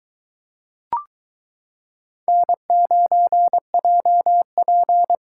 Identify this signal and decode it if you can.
Morse code, heard as a keyed tone, about 23 wpm, 700 Hz. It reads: N9JP